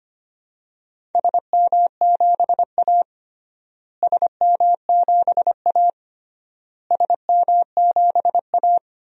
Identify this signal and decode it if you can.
Morse code, heard as a keyed tone, about 25 wpm, 700 Hz. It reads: SM7A SM7A SM7A
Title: DXCC-100 Call Signs - 4 Characters - Repeated 3x - 25wpm